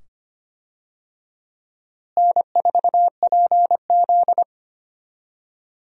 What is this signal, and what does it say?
Morse code, heard as a keyed tone, about 25 wpm, 700 Hz. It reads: N4PZ
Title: USA Call Signs - 4 Characters - Repeated 2x - 25wpm